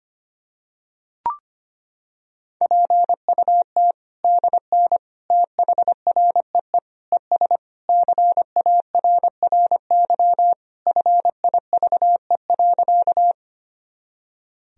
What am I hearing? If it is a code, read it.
Morse code, heard as a keyed tone, about 25 wpm, 700 Hz. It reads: PUT DN THREE ES CARRY FIVE.
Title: Sentences from Top 400 Words and Abbreviations 25wpm